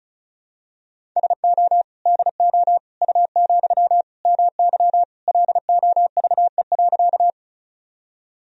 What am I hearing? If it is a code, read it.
Morse code, heard as a keyed tone, about 36 wpm, 700 Hz. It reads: SO DO U, MY LOVE.
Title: Sentences from Top 400 Words and Abbreviations 35wpm